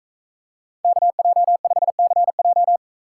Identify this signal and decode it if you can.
Morse code, heard as a keyed tone, about 40 wpm, 700 Hz. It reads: KJ5CJ